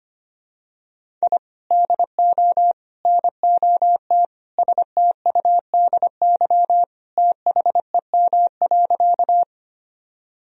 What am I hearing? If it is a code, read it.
Morse code, heard as a keyed tone, about 25 wpm, 700 Hz. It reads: I DO NOT STUDY THEM.
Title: Sentences from Top 1000 Words - Review Part 1 - 25wpm